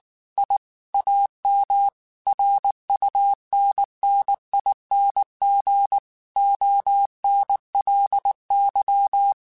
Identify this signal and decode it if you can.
Morse code, heard as a keyed tone, about 19 wpm, 800 Hz. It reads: I AM RUNNING ONLY
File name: QSOs / QSOs-mp3s / kc7haa@18